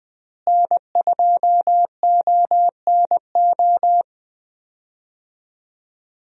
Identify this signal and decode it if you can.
Morse code, heard as a keyed tone, about 20 wpm, 700 Hz. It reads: N2ONO
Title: USA Call Signs - 5 Characters - Repeated 3x - 20wpm